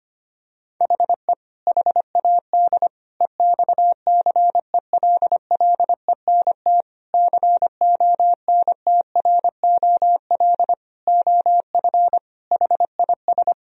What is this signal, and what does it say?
Morse code, heard as a keyed tone, about 25 wpm, 700 Hz. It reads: HE HAD EXCELLENT CONTROL OF HIS
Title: Sentences from Top 1000 Words - Review Part 3 - 25wpm